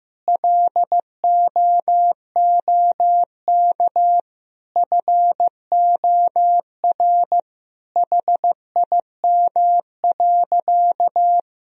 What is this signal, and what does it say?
Morse code, heard as a keyed tone, about 15 wpm, 700 Hz. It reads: LOOK FOR HIM.
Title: Sentences from Top 1000 Words - Review Part 4 - 15wpm